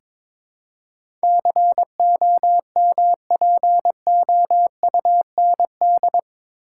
Morse code, heard as a keyed tone, about 22 wpm, 700 Hz. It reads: COMPOUND
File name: practice / 22wpm / Eight Letter Words - 22wpm